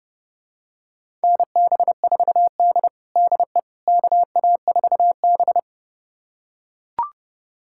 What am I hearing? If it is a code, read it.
Morse code, heard as a keyed tone, about 30 wpm, 700 Hz. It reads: NB4D DE KA4B
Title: USA Callsign DE Callsign - 4 Characters - 30wpm